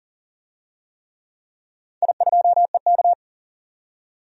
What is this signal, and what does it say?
Morse code, heard as a keyed tone, about 40 wpm, 700 Hz. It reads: I2EK
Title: DXCC-100 Call Signs - 4 Characters - 40wpm